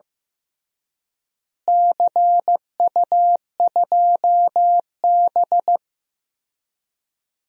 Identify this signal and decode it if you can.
Morse code, heard as a keyed tone, about 15 wpm, 700 Hz. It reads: CU2B
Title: DXCC-100 Call Signs - 4 Characters - Repeated 3x - 15wpm